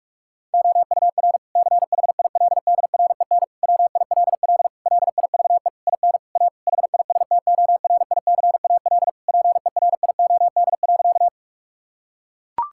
Morse code, heard as a keyed tone, about 41 wpm, 700 Hz. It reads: OUR CHILDREN WILL LIVE IN A HISTORICAL PERIOD.
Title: Sentences from Top 1000 Words - Review Part 5 - 45wpm